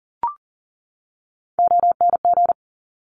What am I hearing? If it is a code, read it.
Morse code, heard as a keyed tone, about 41 wpm, 700 Hz. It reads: ONG
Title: Top Three Letter Combinations in Words 40wpm